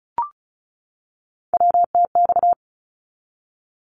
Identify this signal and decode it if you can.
Morse code, heard as a keyed tone, about 34 wpm, 700 Hz. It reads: WTX
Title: ARRL Sections 35wpm